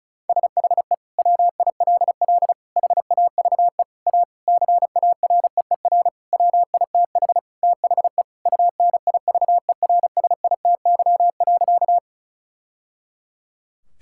Morse code, heard as a keyed tone, about 36 wpm, 700 Hz. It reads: SHE WILL HAVE A CAREER WITH THE UNIVERSITY.
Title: Sentences from Top 1000 Words - Review Part 6 - 35wpm